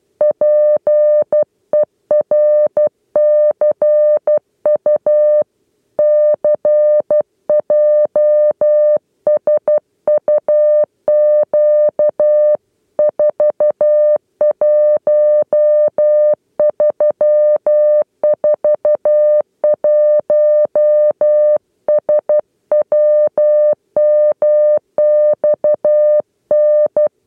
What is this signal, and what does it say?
Morse code, heard as a keyed tone, about 12 wpm, 600 Hz. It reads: PERCU CJSUQ 41341 SWMXN
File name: 8 lezione cw 2